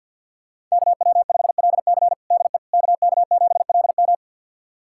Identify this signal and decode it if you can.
Morse code, heard as a keyed tone, about 51 wpm, 700 Hz. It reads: KW5LC DE KK7LM
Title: USA Callsign DE Callsign - 5 Characters - 50wpm